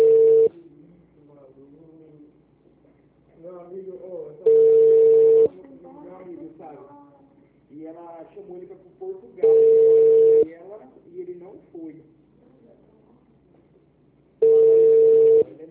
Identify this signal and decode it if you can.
Ringback tone (China, 450 Hz), with speech.